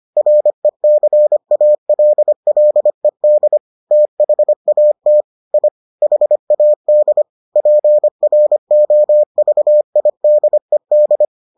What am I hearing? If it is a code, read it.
Morse code, heard as a keyed tone, about 25 wpm, 600 Hz. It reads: RECALLED THAT I HAD PROVIDED